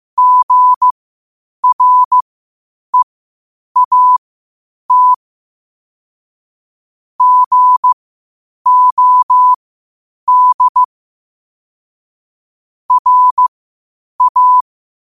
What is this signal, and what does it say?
Morse code, heard as a keyed tone, about 15 wpm, spaced as for 8 wpm, 1000 Hz. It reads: GREAT GOD RA
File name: qotdfiles / 8wpm